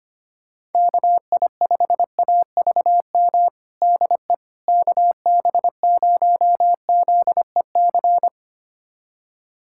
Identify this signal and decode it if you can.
Morse code, heard as a keyed tone, about 25 wpm, 700 Hz. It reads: KI5AVM DE KB0ZEC
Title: USA Callsign DE Callsign - 6 Characters - 25wpm